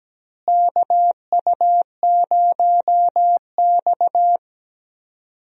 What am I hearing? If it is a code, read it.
Morse code, heard as a keyed tone, about 17 wpm, 700 Hz. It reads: KU0X
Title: USA Call Signs - 4 Characters - Repeated 2x - 17wpm